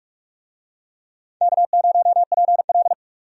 Morse code, heard as a keyed tone, about 43 wpm, 700 Hz. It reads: K0PL
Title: USA Call Signs - 4 Characters - Repeated 3x - 45wpm